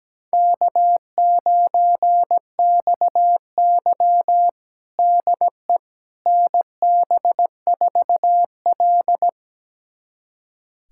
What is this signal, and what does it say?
Morse code, heard as a keyed tone, about 17 wpm, 700 Hz. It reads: K9XY DE NB4L
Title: USA Callsign DE Callsign - 4 Characters - 17wpm